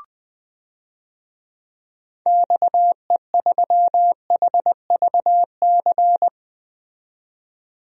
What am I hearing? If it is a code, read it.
Morse code, heard as a keyed tone, about 20 wpm, 700 Hz. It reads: XE3HVC
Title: DXCC-100 Call Signs - 6 Characters - 20wpm